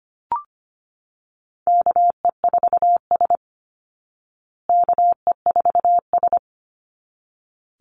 Morse code, heard as a keyed tone, about 25 wpm, 700 Hz. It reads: KE4S KE4S
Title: Call Signs Repeated 3x - 25wpm